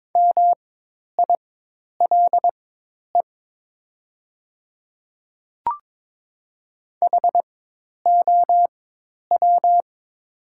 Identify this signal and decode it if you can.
Morse code, heard as a keyed tone, about 22 wpm, spaced as for 10 wpm, 700 Hz. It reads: MILE HOW
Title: 4x Character Spacing - Top 500 Words - 22wpm